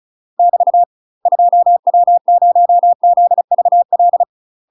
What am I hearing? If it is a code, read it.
Morse code, heard as a keyed tone, about 35 wpm, 700 Hz. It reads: <BT> 2W0ZVL
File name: UK MASTER QSOs TRAINING / 35WPM / UK MASTER QSOs 0004